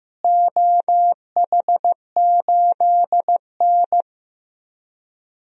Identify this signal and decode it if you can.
Morse code, heard as a keyed tone, about 15 wpm, 700 Hz. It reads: OH8N
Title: DXCC-100 Call Signs - 4 Characters - Repeated 3x - 15wpm